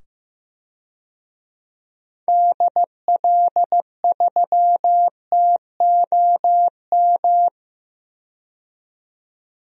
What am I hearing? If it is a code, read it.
Morse code, heard as a keyed tone, about 15 wpm, 700 Hz. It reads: DL3TOM
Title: DXCC-100 Call Signs - 6 Characters - 15wpm